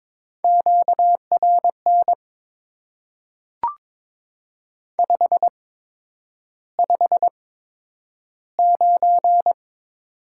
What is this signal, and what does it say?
Morse code, heard as a keyed tone, about 22 wpm, spaced as for 9 wpm, 700 Hz. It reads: QRN 559
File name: Top 90 QSO Elements - 8x Character Spacing - 22wpm